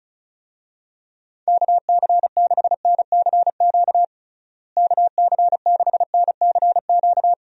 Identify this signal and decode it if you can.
Morse code, heard as a keyed tone, about 36 wpm, 700 Hz. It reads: KC6NCQ KC6NCQ
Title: USA Call Signs - 6 Characters - Repeated 2x - 35wpm